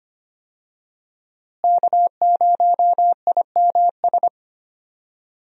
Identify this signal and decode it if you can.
Morse code, heard as a keyed tone, about 25 wpm, 700 Hz. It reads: K0IMS
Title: USA Call Signs - 5 Characters - Repeated 2x - 25wpm